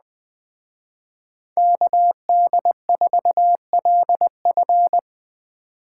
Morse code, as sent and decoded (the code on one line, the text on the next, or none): -.- -.. ....- .-.. ..-.
KD4LF